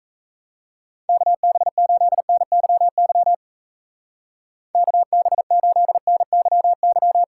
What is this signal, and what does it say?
Morse code, heard as a keyed tone, about 33 wpm, 700 Hz. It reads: KB8NYY KB8NYY
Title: Speed Racing - USA 2x3 Call Signs 28wpm